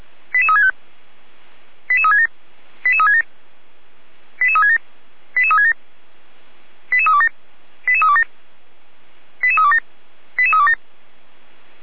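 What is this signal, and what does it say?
Five-tone selective call (ZVEI), heard as quick runs of five tones, tones of about 70 ms. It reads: 88365 88367 88367 88367 88367 88327 88327 88327 88327
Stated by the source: ZVEI five-tone selective call received on an SDR.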